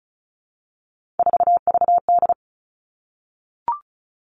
Morse code, heard as a keyed tone, about 36 wpm, 700 Hz. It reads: 4VD
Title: Three Challenging Letter-Number Combinations 35wpm